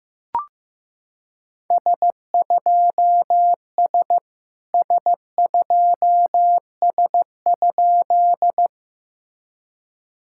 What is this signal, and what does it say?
Morse code, heard as a keyed tone, about 15 wpm, 660 Hz. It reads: S2S S2S?
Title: SOTA QSO Practice 15wpm